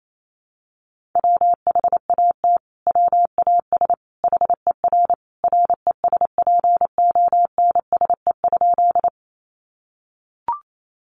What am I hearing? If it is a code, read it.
Morse code, heard as a keyed tone, about 28 wpm, 700 Hz. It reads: WHAT WAS HER RESPONSE?
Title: Sentences from Top 1400 Words 28wpm